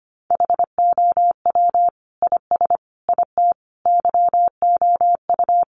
Morse code, heard as a keyed tone, about 25 wpm, 700 Hz. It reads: HOW IS IT YOU
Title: Sentences from Top 700 Words 25wpm